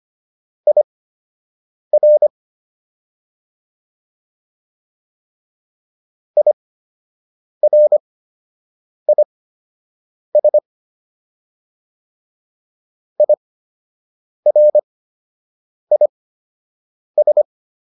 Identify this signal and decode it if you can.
Morse code, heard as a keyed tone, about 25 wpm, spaced as for 7 wpm, 600 Hz. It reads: IR IRIS IRIS